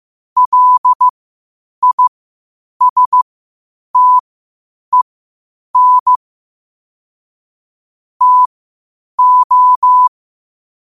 Morse code, heard as a keyed tone, about 15 wpm, spaced as for 8 wpm, 1000 Hz. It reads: LISTEN TO
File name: qotdfiles / 8wpm